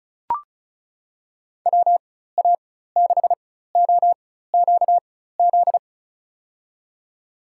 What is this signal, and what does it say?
Morse code, heard as a keyed tone, about 34 wpm, spaced as for 16 wpm, 700 Hz. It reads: WA6OQZ
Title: USA Call Signs - 6 Characters - 4x Character Spacing - 35wpm